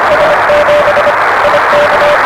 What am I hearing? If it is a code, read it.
Morse code, heard as a keyed tone, about 25 wpm, 560 Hz, with noise. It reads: S7 IK
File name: RS7 Beacon 16 Feb 1985